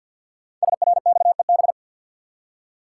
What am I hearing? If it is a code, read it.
Morse code, heard as a keyed tone, about 52 wpm, 700 Hz. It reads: SRXEB
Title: Code Groups - 5 Letters - 50wpm